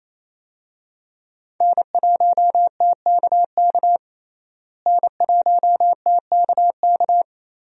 Morse code, heard as a keyed tone, about 28 wpm, 700 Hz. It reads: N1TKK N1TKK
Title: USA Call Signs - 5 Characters - Repeated 3x - 28wpm